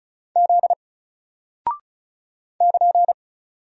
Morse code, heard as a keyed tone, about 35 wpm, 700 Hz. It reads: Z <KN>
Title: Single Character (Letter:Number:Punctuation:Prosign) 35wpm